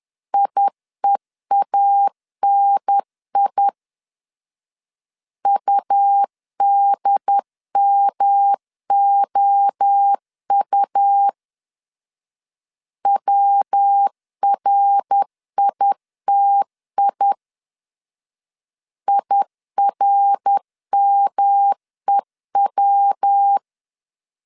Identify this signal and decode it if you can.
Morse code, heard as a keyed tone, about 10 wpm, 800 Hz. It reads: IEANI UDMOU WRITI IRMEW
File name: CW / DET / 1 3 1500C S C 10wpm 3335